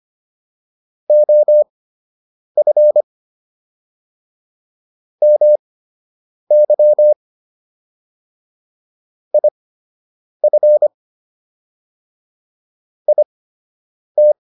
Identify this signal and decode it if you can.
Morse code, heard as a keyed tone, about 25 wpm, spaced as for 8 wpm, 600 Hz. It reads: OF MY IF IT